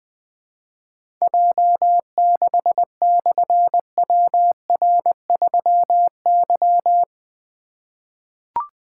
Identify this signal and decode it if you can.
Morse code, heard as a keyed tone, about 20 wpm, 700 Hz. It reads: J6/WR3Y
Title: Difficult Call Signs 20wpm